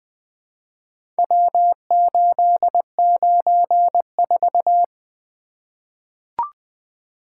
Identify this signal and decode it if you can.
Morse code, heard as a keyed tone, about 20 wpm, 700 Hz. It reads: W894
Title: Four Letter-Number Combinations - 20wpm